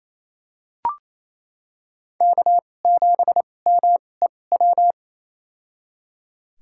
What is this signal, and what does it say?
Morse code, heard as a keyed tone, about 28 wpm, spaced as for 20 wpm, 700 Hz. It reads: K7MEW
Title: USA Call Signs - 5 Characters - 2x Character Spacing - 28wpm